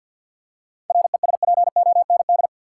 Morse code, heard as a keyed tone, about 45 wpm, 700 Hz. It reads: RESPOND
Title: Seven Letter Words - 50wpm